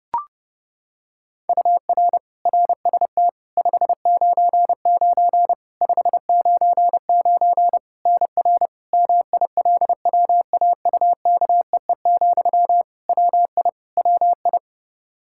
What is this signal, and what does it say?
Morse code, heard as a keyed tone, about 30 wpm, 700 Hz. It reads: UR RST 599 599 NR MILWAUKEE, WI WI
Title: Practice QSO - Exchange 1 - Beginning - 30wpm